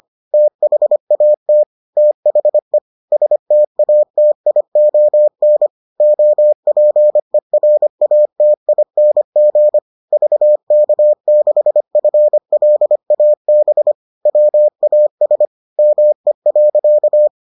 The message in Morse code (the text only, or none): THAT THE STATION OPERATING VK6FLAB WAS ME.